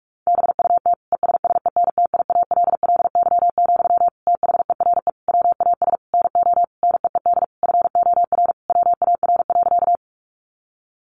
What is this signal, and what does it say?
Morse code, heard as a keyed tone, about 44 wpm, 700 Hz. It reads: BUT ESSENTIALLY, THERE WAS NO NEED FOR WAR.
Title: Sentences from Top 1300 Words 45wpm